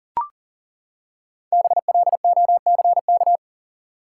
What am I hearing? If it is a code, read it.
Morse code, heard as a keyed tone, about 40 wpm, 700 Hz. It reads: BLOCK
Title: Five Letter Words - 40wpm